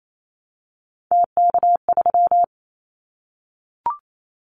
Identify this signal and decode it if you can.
Morse code, heard as a keyed tone, about 28 wpm, 700 Hz. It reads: TK3
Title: Callsign Prefixes for Top 100 DXCC - 28wpm